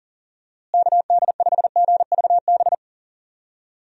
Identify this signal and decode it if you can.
Morse code, heard as a keyed tone, about 40 wpm, 700 Hz. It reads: KD5GVB
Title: USA Call Signs - 6 Characters - Repeated 2x - 40wpm